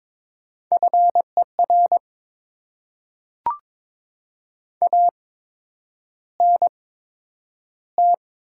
Morse code, heard as a keyed tone, about 22 wpm, spaced as for 9 wpm, 700 Hz. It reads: FER ANT
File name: Top 20 QSO Elements - 8x Character Spacing - 22wpm